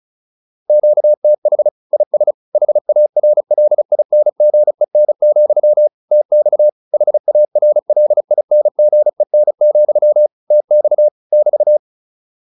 Morse code, heard as a keyed tone, about 34 wpm, 600 Hz. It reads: QTH IS HARLINGEN, TX HARLINGEN, TX <BT>